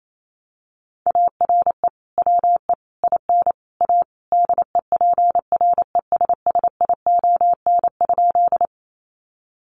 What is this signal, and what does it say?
Morse code, heard as a keyed tone, about 28 wpm, 700 Hz. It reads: ARE WE IN A DEPRESSION?
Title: Sentences from Top 1300 Words 28wpm